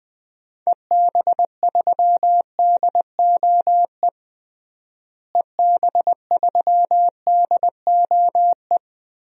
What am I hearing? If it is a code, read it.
Morse code, heard as a keyed tone, about 21 wpm, 700 Hz. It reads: EB3DOE EB3DOE
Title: DXCC-100 Call Signs - 6 Characters - Repeated 3x - 20wpm